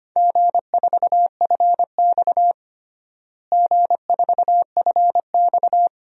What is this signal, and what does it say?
Morse code, heard as a keyed tone, about 25 wpm, 700 Hz. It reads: G4FX G4FX
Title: DXCC-100 Call Signs - 4 Characters - Repeated 2x - 25wpm